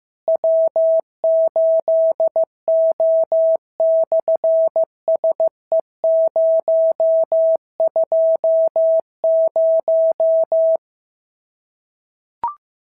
Morse code, heard as a keyed tone, about 15 wpm, 650 Hz. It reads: W8O/SE020
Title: SOTA QSO Practice 15wpm